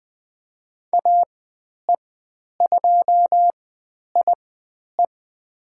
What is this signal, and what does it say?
Morse code, heard as a keyed tone, about 20 wpm, spaced as for 10 wpm, 700 Hz. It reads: AE2IE
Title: 109 Lesson 18g - Letter m - Call Signs 20wpm